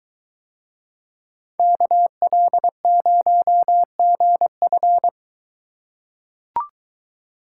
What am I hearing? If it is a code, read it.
Morse code, heard as a keyed tone, about 22 wpm, 700 Hz. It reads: KL0GF